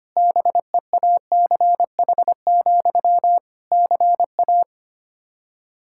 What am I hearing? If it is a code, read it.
Morse code, heard as a keyed tone, about 25 wpm, 700 Hz. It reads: BEACH, CA
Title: Practice QSO - Exchange 1 - Beginning - 25wpm